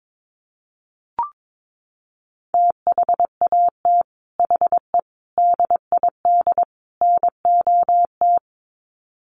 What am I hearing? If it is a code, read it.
Morse code, heard as a keyed tone, about 22 wpm, 700 Hz. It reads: THAT HE DID NOT
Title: Sets of 4 Words 22wpm